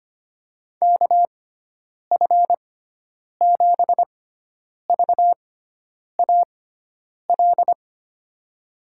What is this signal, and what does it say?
Morse code, heard as a keyed tone, about 25 wpm, spaced as for 9 wpm, 700 Hz. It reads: KF7VAL
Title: USA Call Signs - 6x Character Spacing - 25wpm